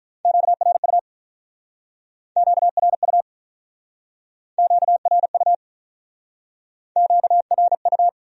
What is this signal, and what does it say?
Morse code, heard as a keyed tone, about 45 wpm, 700 Hz. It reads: QRU QRU QRU QRU